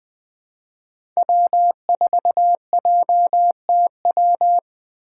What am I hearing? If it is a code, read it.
Morse code, heard as a keyed tone, about 20 wpm, 700 Hz. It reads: W4JTW
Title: USA Call Signs - 2x Character Spacing - 20wpm